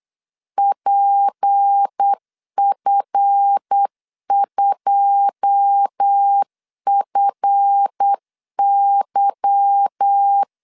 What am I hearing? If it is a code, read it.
Morse code, heard as a keyed tone, about 8 wpm, 800 Hz. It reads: PF2FY
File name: LEZIONI / 9 2 8wpm 3437